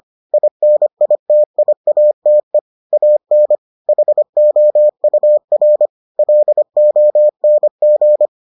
Morse code, heard as a keyed tone, about 25 wpm, 600 Hz. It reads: INITIATE AN HOUR LONG